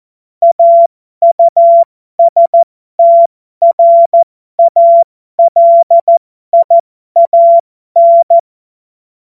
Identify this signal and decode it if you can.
Morse code, heard as a keyed tone, about 14 wpm, spaced as for 12 wpm, 680 Hz. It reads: AUSTRALIAN
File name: Morsekurs / Witze